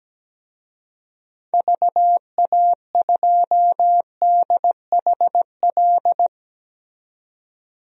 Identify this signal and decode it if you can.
Morse code, heard as a keyed tone, about 17 wpm, 700 Hz. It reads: VA2DHL
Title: DXCC-100 Call Signs - 6 Characters - 17wpm